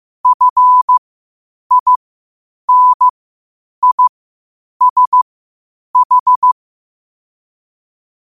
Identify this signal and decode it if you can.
Morse code, heard as a keyed tone, about 15 wpm, spaced as for 8 wpm, 1000 Hz. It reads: FINISH
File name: 8wpm